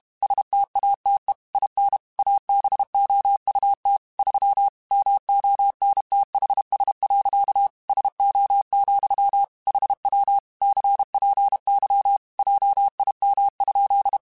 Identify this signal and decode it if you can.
Morse code, heard as a keyed tone, about 32 wpm, 800 Hz. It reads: STATE IN ABOUT 3 MONTHS. SO, HW CPY JIM?
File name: QSOs / QSOs-mp3s / WD0BGP 1@30